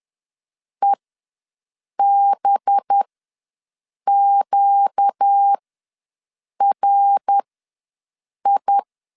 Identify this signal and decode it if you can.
Morse code, heard as a keyed tone, about 10 wpm, spaced as for 6 wpm, 800 Hz. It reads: EBQRI
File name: LEZIONI / 5 LBVC / 5 LB ws 10wpm 3616